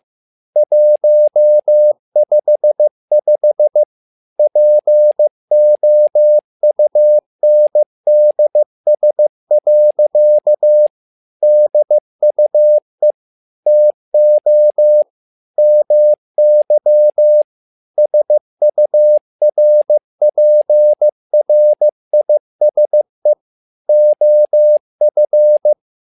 Morse code, heard as a keyed tone, about 15 wpm, 600 Hz. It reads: 155 POUNDS. DUE TO MY SURPRISE OF
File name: learning center / tower trouble / tower trouble 15wpm